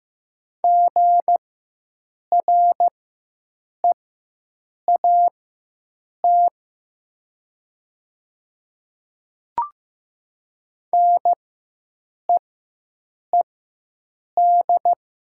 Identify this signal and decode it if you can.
Morse code, heard as a keyed tone, about 15 wpm, spaced as for 7 wpm, 700 Hz. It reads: GREAT NEED
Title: 4x Character Spacing - Top 300 Words - 15wpm